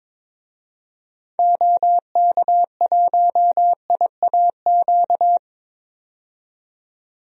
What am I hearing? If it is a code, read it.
Morse code, heard as a keyed tone, about 22 wpm, 700 Hz. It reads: OK1IAQ
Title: DXCC-100 Call Signs - 6 Characters - 22wpm